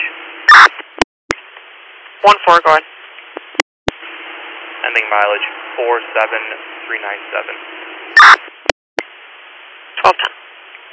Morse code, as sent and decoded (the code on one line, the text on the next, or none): none